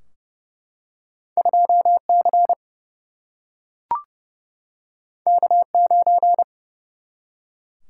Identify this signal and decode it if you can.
Morse code, heard as a keyed tone, about 30 wpm, 700 Hz. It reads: 2C K9
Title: Single-Letter Single-Number 30wpm